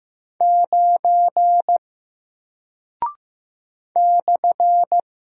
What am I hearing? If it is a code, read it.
Morse code, heard as a keyed tone, about 15 wpm, 700 Hz. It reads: 9 /
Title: Single Character (Letter:Number:Punctuation:Prosign) Alt 15wpm